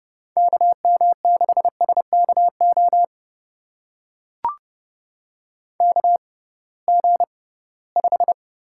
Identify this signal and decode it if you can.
Morse code, heard as a keyed tone, about 30 wpm, spaced as for 19 wpm, 700 Hz. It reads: KM6SKO KG5
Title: USA Call Signs - 6 Characters - 6x Character Spacing - 30wpm